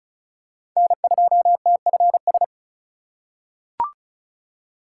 Morse code, heard as a keyed tone, about 36 wpm, 700 Hz. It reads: N2TFS
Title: USA Call Signs - 5 Characters - 2x Character Spacing - 35wpm